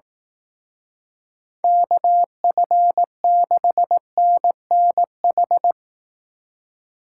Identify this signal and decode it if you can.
Morse code, heard as a keyed tone, about 18 wpm, 700 Hz. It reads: KF6NNH
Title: Speed Racing - USA 2x3 Call Signs 15wpm